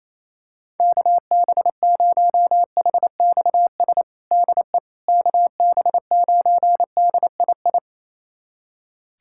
Morse code, heard as a keyed tone, about 28 wpm, 700 Hz. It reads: KB0HXS DE KB9DII
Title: USA Callsign DE Callsign - 6 Characters - 28wpm